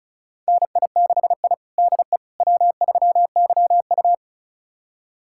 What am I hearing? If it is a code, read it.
Morse code, heard as a keyed tone, about 36 wpm, 700 Hz. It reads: NI6I DE W3YU
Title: USA Callsign DE Callsign - 4 Characters - 35wpm